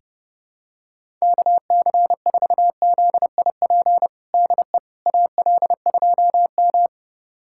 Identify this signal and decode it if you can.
Morse code, heard as a keyed tone, about 30 wpm, 700 Hz. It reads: KC4ZIP DE AL2M
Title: USA Callsign DE Callsign 30wpm